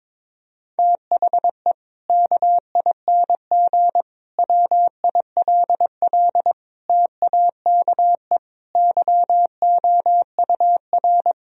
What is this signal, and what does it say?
Morse code, heard as a keyed tone, about 22 wpm, 700 Hz. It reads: THE KING WILL TAKE YOUR
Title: Sentences from Top 1000 Words - Review Part 4 - 22wpm